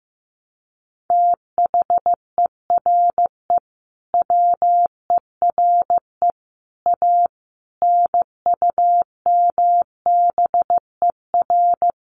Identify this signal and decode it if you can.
Morse code, heard as a keyed tone, about 15 wpm, 700 Hz. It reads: THERE WERE A NUMBER
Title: Sets of 5 Words Encore2 15wpm